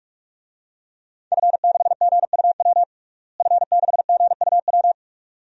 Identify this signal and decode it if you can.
Morse code, heard as a keyed tone, about 46 wpm, 700 Hz. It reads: F6GUW F6GUW
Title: DXCC-100 Call Signs - 5 Characters - Repeated 2x - 45wpm